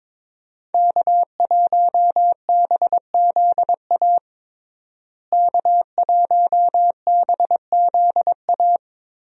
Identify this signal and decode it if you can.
Morse code, heard as a keyed tone, about 22 wpm, 700 Hz. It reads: K1BZA K1BZA
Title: USA Call Signs - 5 Characters - Repeated 3x - 22wpm